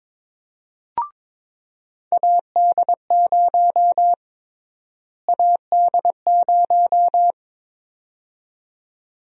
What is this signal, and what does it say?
Morse code, heard as a keyed tone, about 22 wpm, 700 Hz. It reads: AD0 AD0
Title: USA Call Sign Prefixes - Repeated 2x - 22wpm